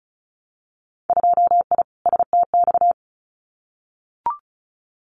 Morse code, heard as a keyed tone, about 34 wpm, 690 Hz. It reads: 2I STX
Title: Winter Field Day Exchanges 35wpm